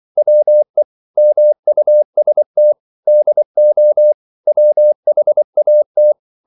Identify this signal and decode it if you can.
Morse code, heard as a keyed tone, about 24 wpm, 600 Hz. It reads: WE MUST DO WHAT
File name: CW / Practice / EnglishQuotations0424